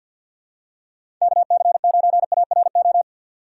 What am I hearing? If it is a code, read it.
Morse code, heard as a keyed tone, about 52 wpm, 700 Hz. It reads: KC9ARO
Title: USA Call Signs - 6 Characters - 4x Character Spacing - 50wpm